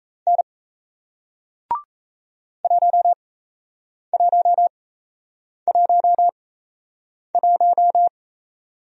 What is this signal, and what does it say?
Morse code, heard as a keyed tone, about 37 wpm, spaced as for 8 wpm, 700 Hz. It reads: N 1111